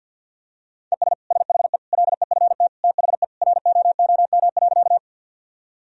Morse code, heard as a keyed tone, about 51 wpm, 700 Hz. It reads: ES SHE LEFT THE ROOM.
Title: Sentences from Top 300 Words and Abbreviations 50wpm